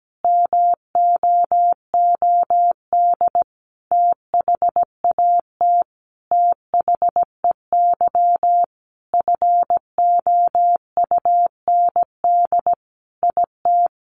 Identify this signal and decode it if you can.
Morse code, heard as a keyed tone, about 17 wpm, 700 Hz. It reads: MOOD THAT THEY FOUND IT
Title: Sentences from Top 1100 Words 17wpm